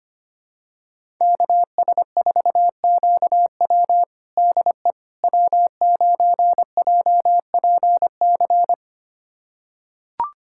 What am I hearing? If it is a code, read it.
Morse code, heard as a keyed tone, about 25 wpm, 700 Hz. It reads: KS4QW DE W9JPC
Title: USA Callsign DE Callsign - 5 Characters - 25wpm